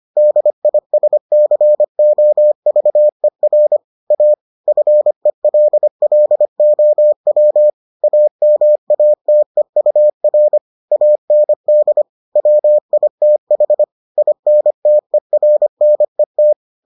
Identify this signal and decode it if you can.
Morse code, heard as a keyed tone, about 25 wpm, 600 Hz. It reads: DISCOVER A FELLOW AMATEUR AND WITH INTERNET